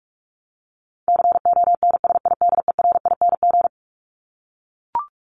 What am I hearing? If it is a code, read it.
Morse code, heard as a keyed tone, about 44 wpm, 700 Hz. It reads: CONSIDERING
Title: ING Suffix - Top 100 Words with Base Words 45wpm